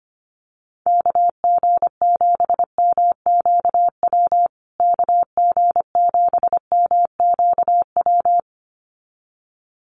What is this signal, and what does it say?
Morse code, heard as a keyed tone, about 25 wpm, 700 Hz. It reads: KG7MQW KG7MQW
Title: USA Call Signs Repeated 2x 25wpm